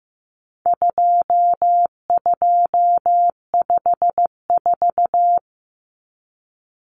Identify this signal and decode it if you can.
Morse code, heard as a keyed tone, about 15 wpm, 700 Hz. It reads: 2254
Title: Common Cut Numbers - 4-Digit 15wpm